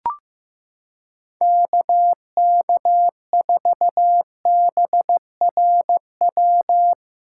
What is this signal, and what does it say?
Morse code, heard as a keyed tone, about 15 wpm, 700 Hz. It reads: KK4BRW
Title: USA Callsign DE Callsign - 6 Characters - 15wpm